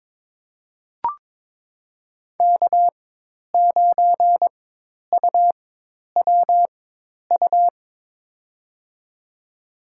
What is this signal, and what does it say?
Morse code, heard as a keyed tone, about 22 wpm, spaced as for 10 wpm, 700 Hz. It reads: K9UWU
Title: USA Call Signs - 5 Characters - 4x Character Spacing - 22wpm